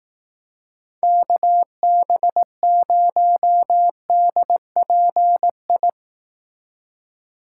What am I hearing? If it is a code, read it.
Morse code, heard as a keyed tone, about 18 wpm, 700 Hz. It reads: KB0DPI
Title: Speed Racing - USA 2x3 Call Signs 15wpm